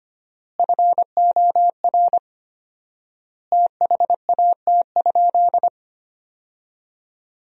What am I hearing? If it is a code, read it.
Morse code, heard as a keyed tone, about 25 wpm, 700 Hz. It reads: FOR THAT?
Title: 4x Word-Spacing - Sentences from Top 300 Words - 25wpm